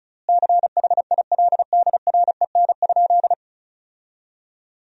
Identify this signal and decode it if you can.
Morse code, heard as a keyed tone, about 36 wpm, 700 Hz. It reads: CHILDREN?
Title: 3x Word-Spacing - Sentences from Top 300 Words - 35wpm